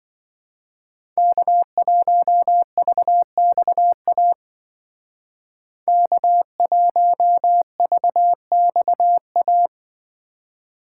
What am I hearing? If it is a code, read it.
Morse code, heard as a keyed tone, about 24 wpm, 700 Hz. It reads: K1VXA K1VXA
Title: Speed Racing - USA 1x3 Call Signs 20wpm